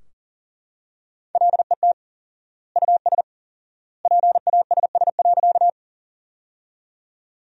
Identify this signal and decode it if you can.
Morse code, heard as a keyed tone, about 40 wpm, 700 Hz. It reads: LET US PASS.
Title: 4x Word-Spacing - Sentences from Top 400 Words - 40wpm